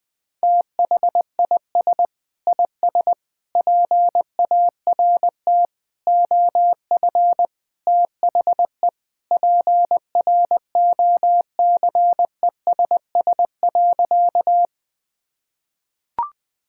Morse code, heard as a keyed tone, about 19 wpm, 700 Hz. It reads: THIS IS PART OF THE PROCESS.
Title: Sentences with 2-Word Phrases - Spoken Beginning - 20wpm